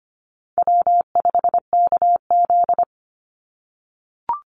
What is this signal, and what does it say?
Morse code, heard as a keyed tone, about 25 wpm, 700 Hz. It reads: W5KZ
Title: USA Call Signs 25wpm